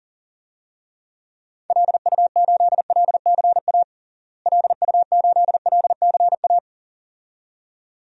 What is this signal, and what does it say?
Morse code, heard as a keyed tone, about 40 wpm, 700 Hz. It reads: LU8LCA LU8LCA
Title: DXCC-100 Call Signs - 6 Characters - Repeated 2x - 40wpm